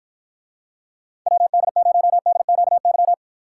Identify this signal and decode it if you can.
Morse code, heard as a keyed tone, about 53 wpm, 700 Hz. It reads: WD0DQY